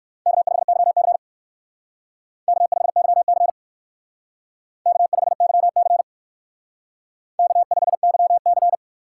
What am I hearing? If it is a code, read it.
Morse code, heard as a keyed tone, about 57 wpm, 700 Hz. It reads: K5YC K5YC K5YC K5YC